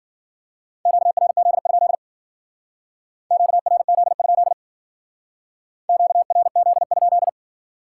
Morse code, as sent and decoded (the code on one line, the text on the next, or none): --.- .-. --.. ..--.. / --.- .-. --.. ..--.. / --.- .-. --.. ..--..
QRZ? QRZ? QRZ?